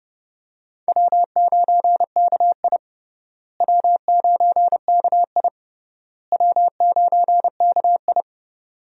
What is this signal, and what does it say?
Morse code, heard as a keyed tone, about 30 wpm, 700 Hz. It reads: W9KI W9KI W9KI
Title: USA Call Signs - 4 Characters - Repeated 3x - 30wpm